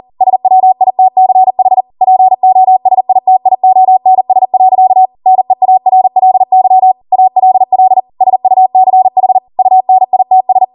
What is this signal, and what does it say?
Morse code, heard as a keyed tone, about 40 wpm, 750 Hz. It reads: SWITCH POSITIONS. NEARLY ALL SUCH UNITS